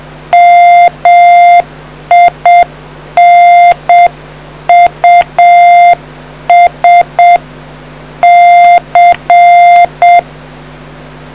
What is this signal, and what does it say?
Morse code, heard as a keyed tone, about 7 wpm, 700 Hz. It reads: MINUS C